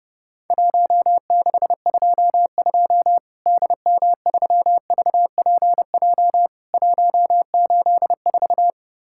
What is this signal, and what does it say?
Morse code, heard as a keyed tone, about 30 wpm, 700 Hz. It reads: 1622 DM3VPJ 184
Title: Worked All Europe DX Contest - QTC Exchange - 30wpm